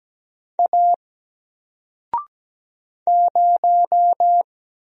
Morse code, heard as a keyed tone, about 17 wpm, 700 Hz. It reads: A 0
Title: Single Letter-Number Alt 17wpm